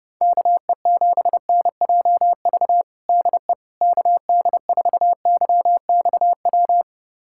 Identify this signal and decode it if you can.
Morse code, heard as a keyed tone, about 32 wpm, 700 Hz. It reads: KE7NJV DE KD4YXW
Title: USA Callsign DE Callsign 30wpm